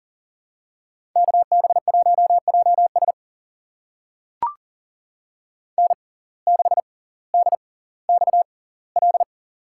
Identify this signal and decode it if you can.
Morse code, heard as a keyed tone, about 38 wpm, spaced as for 20 wpm, 700 Hz. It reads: KB1JS N6DXL
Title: USA Call Signs - 5 Characters - 6x Character Spacing - 40wpm